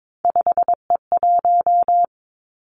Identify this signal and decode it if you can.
Morse code, heard as a keyed tone, about 22 wpm, 700 Hz. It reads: 5E1
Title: Three Challenging Letter-Number Combinations 22wpm